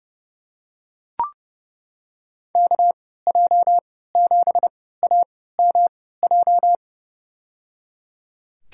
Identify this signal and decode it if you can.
Morse code, heard as a keyed tone, about 30 wpm, spaced as for 17 wpm, 700 Hz. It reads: KJ7AMJ
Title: USA Call Signs - 6 Characters - 3x Character Spacing - 30wpm